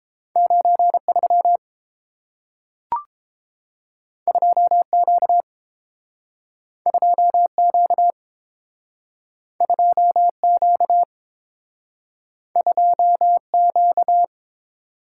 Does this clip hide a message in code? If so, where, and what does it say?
Yes: Morse code from 0.4 to 14.2 s, heard as a keyed tone, about 30 wpm, 700 Hz. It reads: 93 2Q 2Q 2Q 2Q